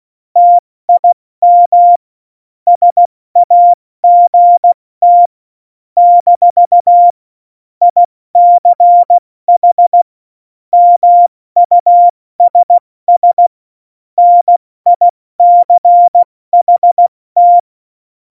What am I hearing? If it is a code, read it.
Morse code, heard as a keyed tone, about 16 wpm, spaced as for 14 wpm, 700 Hz. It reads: TIM SAGT - ICH MUSS NICHT
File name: Morsekurs / Witze / Witz 28. November 2017, 70 F80